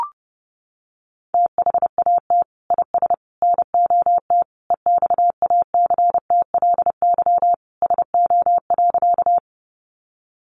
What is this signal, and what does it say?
Morse code, heard as a keyed tone, about 30 wpm, 700 Hz. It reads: THAT IS NOT EXACTLY SO.
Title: Sentences from Top 900 Words 30wpm